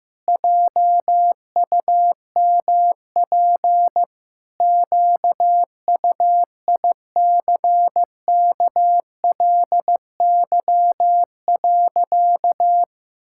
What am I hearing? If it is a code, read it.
Morse code, heard as a keyed tone, about 15 wpm, 700 Hz. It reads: JUMP QUICKLY.
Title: Pangram Challenge v2 - 15wpm